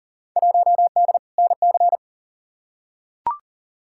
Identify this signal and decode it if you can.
Morse code, heard as a keyed tone, about 40 wpm, 700 Hz. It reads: 1D NC
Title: Field Day Exchanges - 40wpm